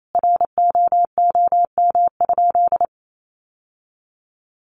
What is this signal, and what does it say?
Morse code, heard as a keyed tone, about 28 wpm, 700 Hz. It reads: ROOM?
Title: Sentences from Top 1300 Words 28wpm